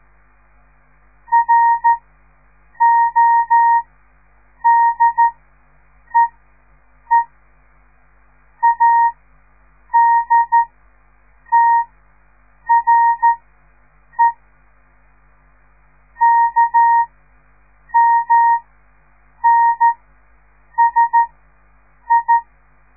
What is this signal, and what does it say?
Morse code, heard as a keyed tone, about 13 wpm, spaced as for 7 wpm, 930 Hz. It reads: RODEE ADTRE KMNSI
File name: radiomorningstar / morse / les 6